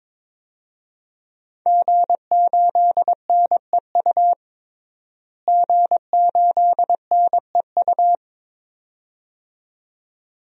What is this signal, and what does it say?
Morse code, heard as a keyed tone, about 22 wpm, 700 Hz. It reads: G8NEU G8NEU
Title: DXCC-100 Call Signs - 5 Characters - Repeated 2x - 22wpm